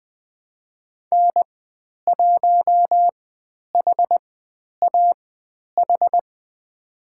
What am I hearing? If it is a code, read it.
Morse code, heard as a keyed tone, about 20 wpm, spaced as for 10 wpm, 700 Hz. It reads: N1HAH
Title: 109 Lesson 18g - Letter m - Call Signs 20wpm